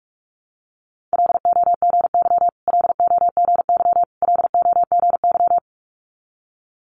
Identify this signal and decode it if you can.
Morse code, heard as a keyed tone, about 46 wpm, 700 Hz. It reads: LOGY LOGY LOGY
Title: Multi-letter Phonemes Repeated 3x 45wpm